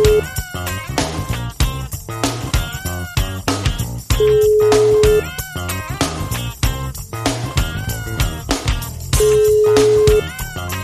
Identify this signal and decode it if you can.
Ringback tone (Europe, 425 Hz), with noise.